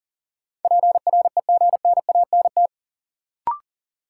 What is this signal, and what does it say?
Morse code, heard as a keyed tone, about 40 wpm, 700 Hz. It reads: PREGNANT
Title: Sentences from Top 1000 Words - Review Part 6 - New Word Drill - 40wpm